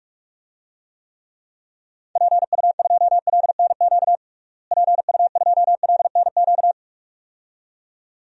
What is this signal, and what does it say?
Morse code, heard as a keyed tone, about 46 wpm, 700 Hz. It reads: PU2LNQ PU2LNQ
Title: DXCC-100 Call Signs - 6 Characters - Repeated 2x - 45wpm